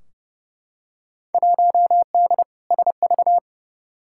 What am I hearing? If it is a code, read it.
Morse code, heard as a keyed tone, about 29 wpm, 700 Hz. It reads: 1D SV
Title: Field Day Exchanges - 30wpm